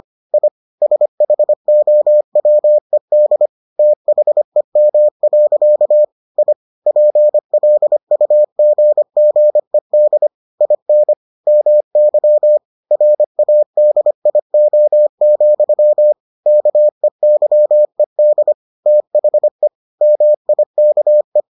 Morse code, heard as a keyed tone, about 25 wpm, 600 Hz. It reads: I SHOWED THEM. I PLUGGED IN MY RADIO, KEYED THE MIKE